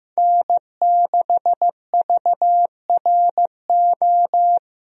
Morse code, heard as a keyed tone, about 15 wpm, 700 Hz. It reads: N6VRO
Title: USA Call Signs - 5 Characters - Repeated 3x - 15wpm